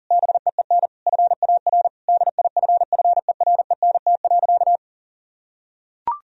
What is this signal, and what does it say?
Morse code, heard as a keyed tone, about 40 wpm, 700 Hz. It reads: BEEN FAR DIFFERENT.
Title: Sentences from Top 1000 Words - Review Part 3 - 40wpm